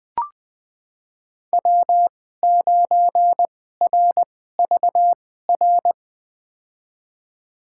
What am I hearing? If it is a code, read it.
Morse code, heard as a keyed tone, about 20 wpm, spaced as for 14 wpm, 700 Hz. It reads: W9RVR
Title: USA Call Signs - 5 Characters - 2x Character Spacing - 20wpm